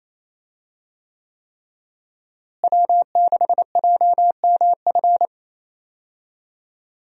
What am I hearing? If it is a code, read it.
Morse code, heard as a keyed tone, about 27 wpm, 700 Hz. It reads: W6JMF
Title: USA Call Signs - 5 Characters - Repeated 3x - 28wpm